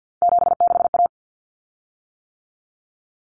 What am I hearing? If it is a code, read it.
Morse code, heard as a keyed tone, about 50 wpm, 700 Hz. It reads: 76A
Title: Common Cut Numbers - 3-Digit 50wpm